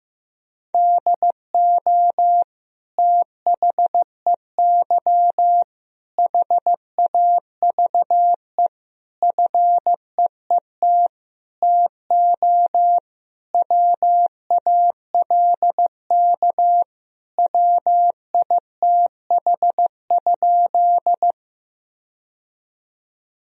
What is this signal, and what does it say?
Morse code, heard as a keyed tone, about 15 wpm, 700 Hz. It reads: DO THEY HAVE FEET TO WALK WITH?
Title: Sentences from Top 1000 Words - Review Part 1 - 15wpm